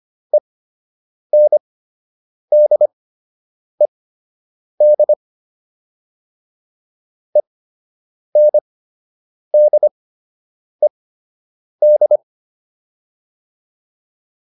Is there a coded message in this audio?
Yes: Morse code from 0.3 to 12.2 s, heard as a keyed tone, about 25 wpm, spaced as for 8 wpm, 600 Hz. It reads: ENDED ENDED